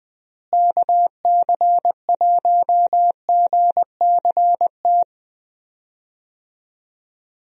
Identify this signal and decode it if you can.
Morse code, heard as a keyed tone, about 20 wpm, 700 Hz. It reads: KC1GCT
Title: USA Call Signs - 6 Characters - Repeated 3x - 20wpm